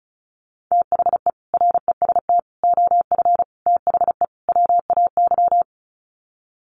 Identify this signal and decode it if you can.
Morse code, heard as a keyed tone, about 34 wpm, 700 Hz. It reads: THE REST OF THE WAY
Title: Sets of 5 Words Encore 35wpm